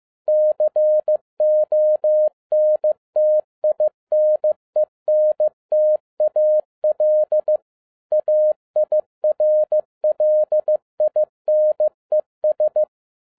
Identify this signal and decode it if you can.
Morse code, heard as a keyed tone, about 15 wpm, 610 Hz. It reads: CONTINENTAL AIRLINES
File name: 2011-07-mp3 / n12 T8XO 15wpm